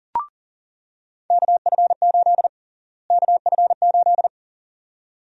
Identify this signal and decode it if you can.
Morse code, heard as a keyed tone, about 37 wpm, 700 Hz. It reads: KF8 KF8
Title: USA Call Sign Prefixes - Repeated 2x - 40wpm